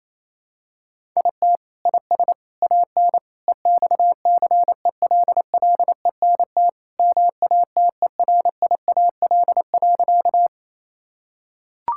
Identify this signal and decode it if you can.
Morse code, heard as a keyed tone, about 28 wpm, 700 Hz. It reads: IT IS AN EXCELLENT MATERIAL.
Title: Sentences from Top 1000 Words - Review Part 5 - 28wpm